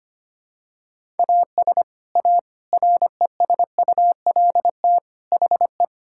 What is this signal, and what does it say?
Morse code, heard as a keyed tone, about 25 wpm, 700 Hz. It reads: AS A RESULT HE
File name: practice / 25wpm / Sentences with 3-Word Phrases - 25wpm